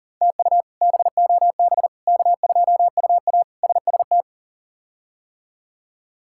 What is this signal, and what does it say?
Morse code, heard as a keyed tone, about 40 wpm, 700 Hz. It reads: TU BOB K2UA SST
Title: K1USN SST QSO Practice - 40wpm